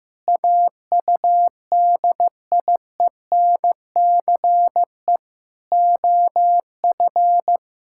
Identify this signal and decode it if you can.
Morse code, heard as a keyed tone, about 15 wpm, 700 Hz. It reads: AUDIENCE OF
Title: Sentences from Top 1000 Words - Review Part 5 - 15wpm